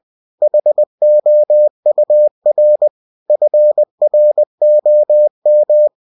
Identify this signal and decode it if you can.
Morse code, heard as a keyed tone, about 20 wpm, 600 Hz. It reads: HOUR FROM
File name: cw-books / winnie-the-pooh / winnie-20wpm-page-0041